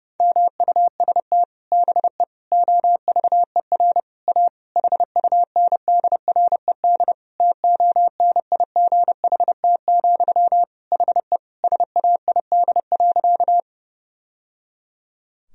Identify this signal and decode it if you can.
Morse code, heard as a keyed tone, about 30 wpm, 700 Hz. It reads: MUST BE OVER A HUNDRED TONIGHT, HE SAID.
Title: Sentences from Top 1000 Words - Review Part 2 - 30wpm